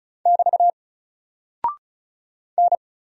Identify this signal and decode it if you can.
Morse code, heard as a keyed tone, about 36 wpm, 700 Hz. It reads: <BT> N
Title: Single Character (Letter:Number:Punctuation:Prosign) 35wpm